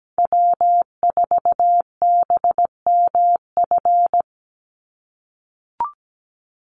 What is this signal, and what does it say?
Morse code, heard as a keyed tone, about 17 wpm, 700 Hz. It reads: W4BMF
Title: USA Call Signs 17wpm